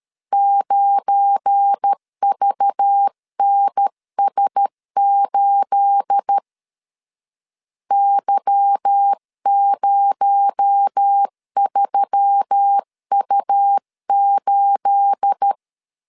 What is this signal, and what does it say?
Morse code, heard as a keyed tone, about 13 wpm, 800 Hz. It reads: 9VNS8 Y03U8
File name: LEZIONI / 9 28 12wpm 2351